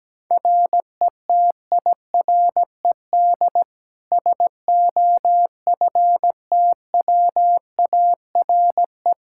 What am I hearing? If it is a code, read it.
Morse code, heard as a keyed tone, about 17 wpm, 700 Hz. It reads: RETIRED SOFTWARE
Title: Practice QSO - Exchange 3 - More Info - 17wpm